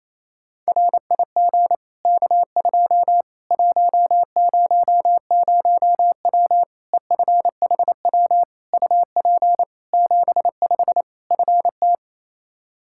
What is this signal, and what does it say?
Morse code, heard as a keyed tone, about 28 wpm, 700 Hz. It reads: RIG K2 100W EFHW UP 75 FT
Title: Practice QSO - Exchange 2 - Info - 28wpm